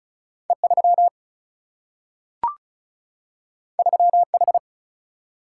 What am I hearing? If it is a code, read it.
Morse code, heard as a keyed tone, about 36 wpm, 700 Hz. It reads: E3 3H
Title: Single-Letter Single-Number 35wpm